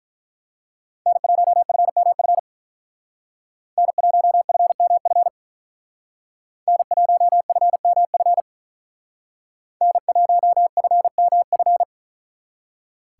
Morse code, heard as a keyed tone, about 43 wpm, 700 Hz. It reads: N1FMF N1FMF N1FMF N1FMF
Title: Speed Racing - USA Call Signs 35wpm